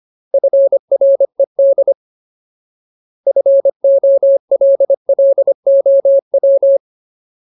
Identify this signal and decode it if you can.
Morse code, heard as a keyed tone, about 25 wpm, 550 Hz. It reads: FRED FOLLOW
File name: Ultimate Word List / 25WPM Ultimate